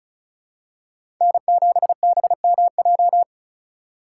Morse code, heard as a keyed tone, about 36 wpm, 700 Hz. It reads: N7BMJ
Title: USA Call Signs - 5 Characters - 2x Character Spacing - 35wpm